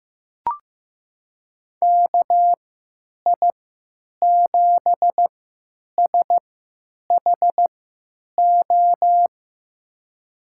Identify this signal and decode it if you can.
Morse code, heard as a keyed tone, about 15 wpm, spaced as for 8 wpm, 700 Hz. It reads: KI7SHO
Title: USA Call Signs - 6 Characters - 3x Character Spacing - 15wpm